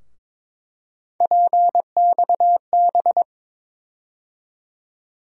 Morse code, heard as a keyed tone, about 22 wpm, 700 Hz. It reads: PXB
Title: USA Call Sign Postfixes - 3 Letters 22wpm